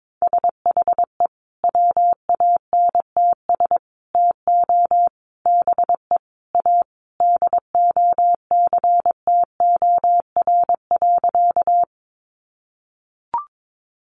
Morse code, heard as a keyed tone, about 22 wpm, 700 Hz. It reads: SHE WANTS TO BE A DOCTOR.
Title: Easy Phrases 22wpm